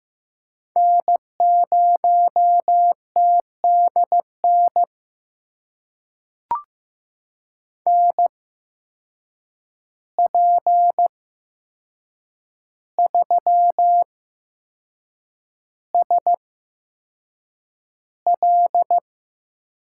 Morse code, heard as a keyed tone, about 15 wpm, spaced as for 6 wpm, 700 Hz. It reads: N0TDN NP3SL
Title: USA Call Signs - 5 Characters - 8x Character Spacing - 15wpm